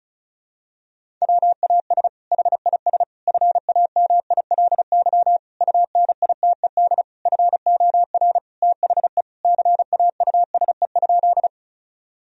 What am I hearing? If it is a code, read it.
Morse code, heard as a keyed tone, about 34 wpm, 700 Hz. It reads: WAS HIS FAMILY UNITED FOR THE CAUSE?
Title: Sentences from Top 1000 Words - Review Part 4 - 35wpm